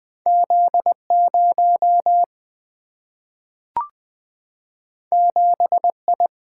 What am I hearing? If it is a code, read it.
Morse code, heard as a keyed tone, about 20 wpm, 700 Hz. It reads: Z0 7I
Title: Single-Letter Single-Number 20wpm